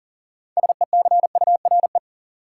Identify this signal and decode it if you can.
Morse code, heard as a keyed tone, about 37 wpm, 700 Hz. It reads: SECURE
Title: Sentences from Top 1200 Words - New Word Drill Alt - 40wpm